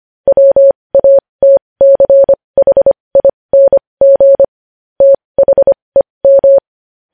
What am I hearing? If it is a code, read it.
Morse code, heard as a keyed tone, about 25 wpm, spaced as for 20 wpm, 550 Hz. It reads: WATCHING THEM
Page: news-headlines-in-morse-code-at-20-wpm